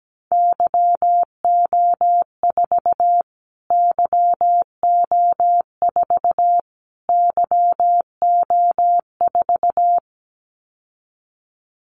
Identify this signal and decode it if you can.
Morse code, heard as a keyed tone, about 17 wpm, 700 Hz. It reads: YO4 YO4 YO4
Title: Callsign Prefixes for Top 25 DXCC - Repeated 3x - 17wpm